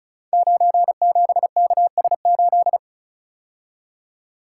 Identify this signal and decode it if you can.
Morse code, heard as a keyed tone, about 36 wpm, 700 Hz. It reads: 97KS8
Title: Code Groups - 5 Letter:Numbers - 35wpm